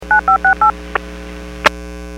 DTMF, heard as telephone touch tones, tones of about 110 ms. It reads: 9568
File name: WPS / 00871 13-11 WPS 451.25 WPS